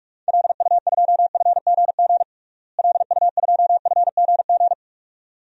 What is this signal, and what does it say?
Morse code, heard as a keyed tone, about 46 wpm, 700 Hz. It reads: LU2FGG LU2FGG
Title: DXCC-100 Call Signs - 6 Characters - Repeated 2x - 45wpm